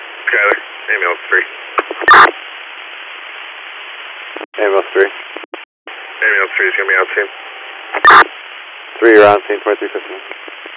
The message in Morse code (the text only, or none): none